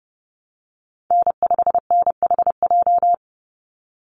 Morse code, heard as a keyed tone, about 30 wpm, 700 Hz. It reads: N5NHJ
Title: Call Signs Repeated 3x - 30wpm